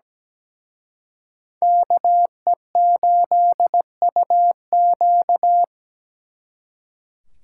Morse code, heard as a keyed tone, about 20 wpm, 700 Hz. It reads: KE8UQ
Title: Speed Racing - USA Call Signs 17wpm